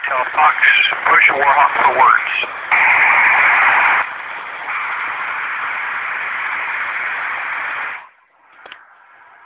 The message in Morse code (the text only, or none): none